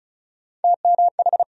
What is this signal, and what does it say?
Morse code, heard as a keyed tone, about 34 wpm, 700 Hz. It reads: TMH
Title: USA Call Sign Postfixes - 3 Letters 35wpm